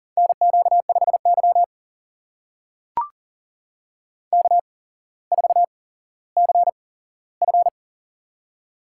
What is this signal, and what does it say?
Morse code, heard as a keyed tone, about 40 wpm, spaced as for 17 wpm, 700 Hz. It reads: NQ5Y K4CF
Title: USA Call Signs - 4 Characters - 8x Character Spacing - 40wpm